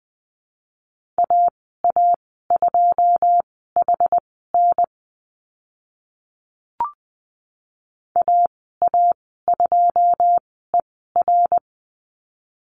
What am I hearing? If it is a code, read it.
Morse code, heard as a keyed tone, about 20 wpm, spaced as for 14 wpm, 700 Hz. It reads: AA2HN AA2ER
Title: 082 Lesson 14g - Number 2 - Call Signs 20wpm